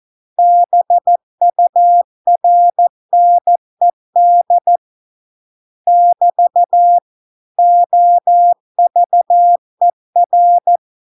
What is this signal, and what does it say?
Morse code, heard as a keyed tone, about 14 wpm, 700 Hz. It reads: BURNED <BT> OVER